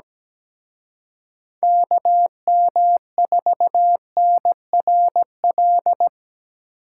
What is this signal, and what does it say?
Morse code, heard as a keyed tone, about 19 wpm, 700 Hz. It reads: KM4NRL
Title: Speed Racing - USA 2x3 Call Signs 17wpm